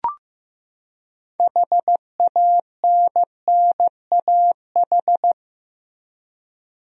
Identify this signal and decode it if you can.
Morse code, heard as a keyed tone, about 15 wpm, 700 Hz. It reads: HANNAH
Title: USA Common Names - 15wpm